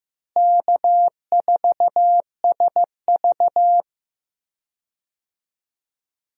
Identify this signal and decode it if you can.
Morse code, heard as a keyed tone, about 16 wpm, 700 Hz. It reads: K4SV
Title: USA Call Signs - 4 Characters - Repeated 2x - 15wpm